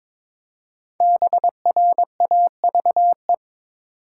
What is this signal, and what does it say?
Morse code, heard as a keyed tone, about 22 wpm, 700 Hz. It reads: BRAVE
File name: 5 Letter Words - First 3 Letters Given - 22wpm